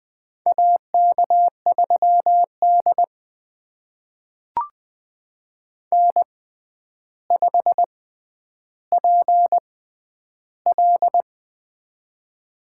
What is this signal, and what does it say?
Morse code, heard as a keyed tone, about 20 wpm, spaced as for 10 wpm, 700 Hz. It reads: AK3D N5PL
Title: USA Call Signs - 4 Characters - 6x Character Spacing - 20wpm